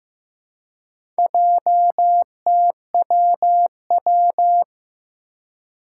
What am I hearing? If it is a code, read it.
Morse code, heard as a keyed tone, about 15 wpm, 700 Hz. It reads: JTWW
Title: Four Challenging Letter-Number Combinations 15wpm